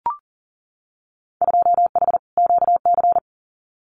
Morse code, heard as a keyed tone, about 40 wpm, 700 Hz. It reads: 2H QC
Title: Winter Field Day Exchanges 40wpm